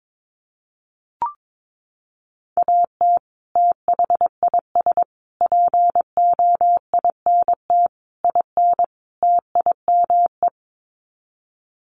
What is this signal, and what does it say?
Morse code, heard as a keyed tone, about 22 wpm, 700 Hz. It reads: AT THIS POINT IN TIME
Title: Sets of 5 Words Encore 22wpm